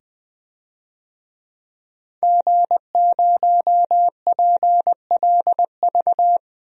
Morse code, heard as a keyed tone, about 20 wpm, 700 Hz. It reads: G0PLV
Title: DXCC-100 Call Signs - 5 Characters - 20wpm